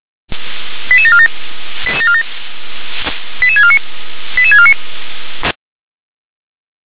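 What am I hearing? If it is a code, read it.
Five-tone selective call (ZVEI), heard as quick runs of five tones, tones of about 70 ms. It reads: 88747 88649 88649
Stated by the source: ZVEI five-tone selective call 88783 received by SDR.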